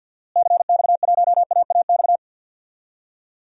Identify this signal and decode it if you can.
Morse code, heard as a keyed tone, about 52 wpm, 700 Hz. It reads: CX1AAX
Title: DXCC-100 Call Signs - 6 Characters - Repeated 3x - 50wpm